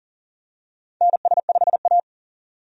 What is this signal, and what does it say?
Morse code, heard as a keyed tone, about 40 wpm, 700 Hz. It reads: NS5A
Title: USA Call Signs - 4 Characters - 40wpm